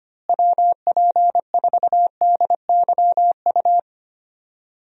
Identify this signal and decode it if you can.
Morse code, heard as a keyed tone, about 25 wpm, 700 Hz. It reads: WP4DYU
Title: USA Call Signs - 6 Characters - Repeated 2x - 25wpm